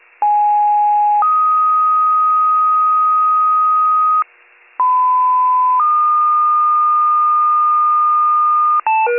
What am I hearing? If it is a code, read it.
Two-tone page, heard as a one-second tone, then a three-second tone, about 830 then 1250 Hz, about 1010 then 1220 Hz. It is heard from 0.2 to 8.8 s.